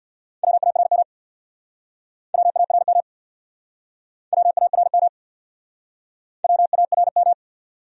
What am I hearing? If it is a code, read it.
Morse code, heard as a keyed tone, about 65 wpm, 700 Hz. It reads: WARM WARM WARM WARM